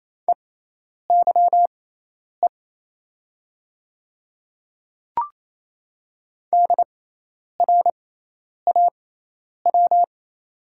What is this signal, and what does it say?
Morse code, heard as a keyed tone, about 28 wpm, spaced as for 10 wpm, 700 Hz. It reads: EYE DRAW
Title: 6x Character Spacing - Top 300 Words - 28wpm